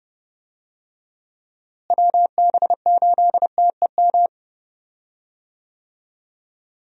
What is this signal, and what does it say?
Morse code, heard as a keyed tone, about 30 wpm, 700 Hz. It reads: WB8TEM
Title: USA Call Signs - 6 Characters - Repeated 3x - 30wpm